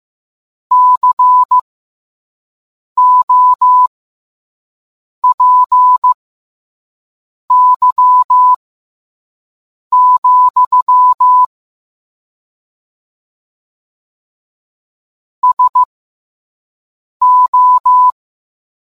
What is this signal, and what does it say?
Morse code, heard as a keyed tone, about 15 wpm, spaced as for 5 wpm, 1000 Hz. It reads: COPY, SO